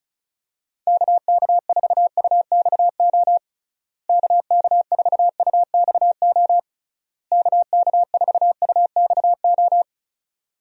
Morse code, heard as a keyed tone, about 34 wpm, 700 Hz. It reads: KK4UXO KK4UXO KK4UXO
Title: USA Call Signs - 6 Characters - Repeated 3x - 35wpm